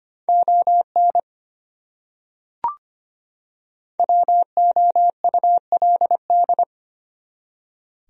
Morse code, heard as a keyed tone, about 25 wpm, 700 Hz. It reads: ON WOULD
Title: Top 100 Words 25wpm